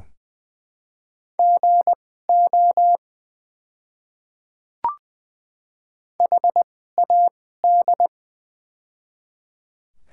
Morse code, heard as a keyed tone, about 20 wpm, spaced as for 15 wpm, 700 Hz. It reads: GO HAD
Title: 2x Character Spacing - Top 100 Words - 20wpm